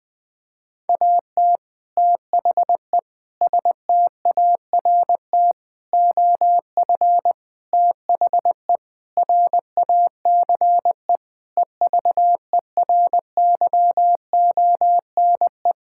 Morse code, heard as a keyed tone, about 20 wpm, 700 Hz. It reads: AT THE START OF THE RACE EVERYONE
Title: Sentences with 5-Word Phrases - Spoken Beginning - 20wpm